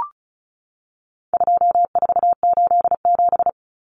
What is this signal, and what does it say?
Morse code, heard as a keyed tone, about 34 wpm, 700 Hz. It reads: 2487
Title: Numbers - 4-Digit 35wpm